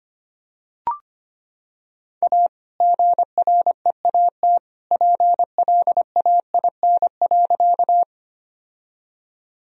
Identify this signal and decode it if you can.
Morse code, heard as a keyed tone, about 25 wpm, 700 Hz. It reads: A GREAT PLAIN.
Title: Sentences from Top 1000 Words - Review Part 2 - 25wpm